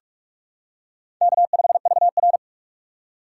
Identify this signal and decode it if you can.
Morse code, heard as a keyed tone, about 46 wpm, 700 Hz. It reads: K5VR
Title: K1USN SST QSO Practice - 45wpm